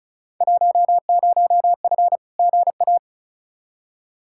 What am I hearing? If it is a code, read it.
Morse code, heard as a keyed tone, about 36 wpm, 700 Hz. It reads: 10F GA
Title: Field Day Exchanges Encore - 35wpm